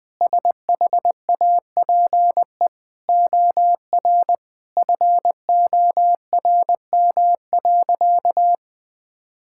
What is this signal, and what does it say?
Morse code, heard as a keyed tone, about 20 wpm, 700 Hz. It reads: SHAPE OR FORM.
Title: Sentences with 6-Word Phrases - Spoken Beginning - 20wpm